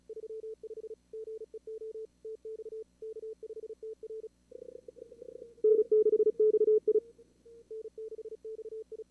Morse code, heard as a keyed tone, about 40 wpm, 440 Hz. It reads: 35 GEO TX K5TR N6XI N6XI
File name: CWSprint / Feb2022 / k5tr-feb-na-sprint-cw-k5ot -2300